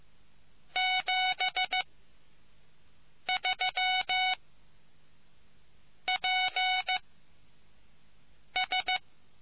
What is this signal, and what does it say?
Morse code, heard as a keyed tone, about 15 wpm, 740 Hz. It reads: 7 3 P S